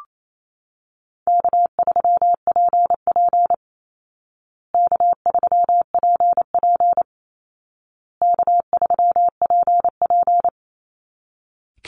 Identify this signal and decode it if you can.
Morse code, heard as a keyed tone, about 28 wpm, 700 Hz. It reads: K3PP K3PP K3PP
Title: Easy Call Signs Repeated 3x - 28wpm